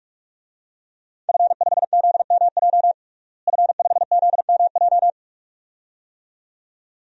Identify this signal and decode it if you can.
Morse code, heard as a keyed tone, about 46 wpm, 700 Hz. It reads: F5ZMJ F5ZMJ
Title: DXCC-100 Call Signs - 5 Characters - Repeated 2x - 45wpm